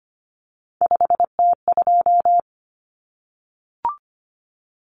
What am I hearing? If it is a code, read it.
Morse code, heard as a keyed tone, about 25 wpm, 700 Hz. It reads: 5T2
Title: Common Cut Numbers - 3-Digit 25wpm